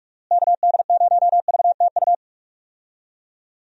Morse code, heard as a keyed tone, about 41 wpm, 700 Hz. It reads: KD0VTU
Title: USA Call Signs - 6 Characters - Repeated 3x - 45wpm